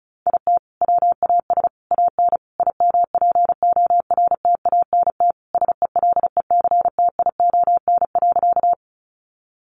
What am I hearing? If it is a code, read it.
Morse code, heard as a keyed tone, about 34 wpm, 700 Hz. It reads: IT WAS AN IMPORTANT SELECTION.
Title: Sentences from Top 1400 Words 35wpm